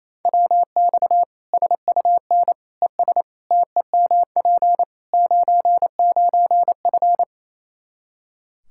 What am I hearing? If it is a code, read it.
Morse code, heard as a keyed tone, about 28 wpm, 700 Hz. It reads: WX SUN ES TEMP 99F
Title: Practice QSO - Exchange 2 - Info - 28wpm